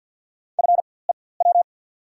Morse code, heard as a keyed tone, about 49 wpm, spaced as for 23 wpm, 700 Hz. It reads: FEW
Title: 4x Character Spacing - Top 500 Words - 50wpm